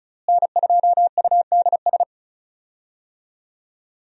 Morse code, heard as a keyed tone, about 36 wpm, 700 Hz. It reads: N2UDS
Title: USA Call Signs - 5 Characters - Repeated 2x - 35wpm